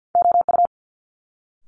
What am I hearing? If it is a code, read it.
Morse code, heard as a keyed tone, about 50 wpm, 700 Hz. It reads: OU
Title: Multi-letter Phonemes Quick Fire 50wpm